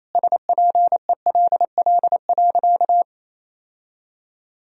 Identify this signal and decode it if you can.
Morse code, heard as a keyed tone, about 28 wpm, 700 Hz. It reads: SPELL.
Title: Sentences from Top 1000 Words - Review Part 1 - 28wpm